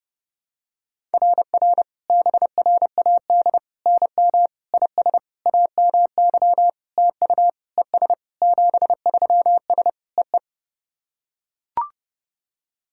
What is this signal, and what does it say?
Morse code, heard as a keyed tone, about 30 wpm, 700 Hz. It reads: RR BRAD NM IS AMY TU ES 73S EE
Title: SOTA QSO Practice 30wpm